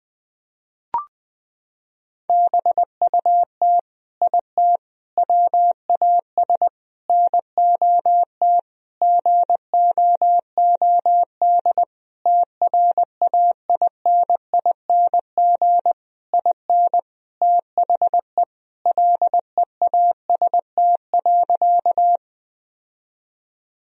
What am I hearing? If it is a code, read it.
Morse code, heard as a keyed tone, about 20 wpm, 700 Hz. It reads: BUT IT WAS NOT GOOD TRAINING IN THE LEAST.
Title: Sentences from Top 1000 Words - Review Part 4 - 20wpm